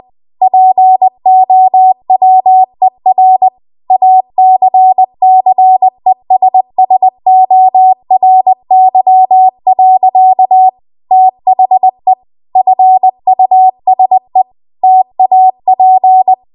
Morse code, heard as a keyed tone, about 20 wpm, 750 Hz. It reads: POWER ACCESSORY. THE FUSE TAP
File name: Archive / 20 WPM / 20WPM